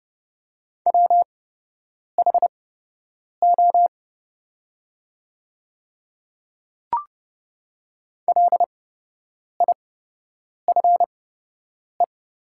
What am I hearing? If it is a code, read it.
Morse code, heard as a keyed tone, about 30 wpm, spaced as for 8 wpm, 700 Hz. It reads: WHO LIFE
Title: 8x Character Spacing - Top 500 Words - 30wpm